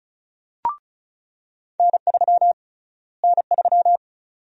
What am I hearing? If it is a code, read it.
Morse code, heard as a keyed tone, about 34 wpm, 700 Hz. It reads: N3 N3
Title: USA Call Sign Prefixes - Repeated 2x - 35wpm